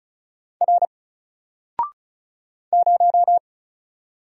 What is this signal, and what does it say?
Morse code, heard as a keyed tone, about 36 wpm, 700 Hz. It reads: R 0
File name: Single-Letter-Number-35wpm